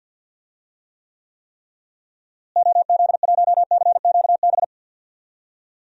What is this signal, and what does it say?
Morse code, heard as a keyed tone, about 50 wpm, 700 Hz. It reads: OZ1CQB
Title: DXCC-100 Call Signs - 6 Characters - Repeated 3x - 50wpm